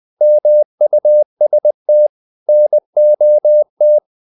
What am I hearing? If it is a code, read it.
Morse code, heard as a keyed tone, about 20 wpm, 600 Hz. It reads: MUST NOT
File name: the old man and the sea part 2 20wpm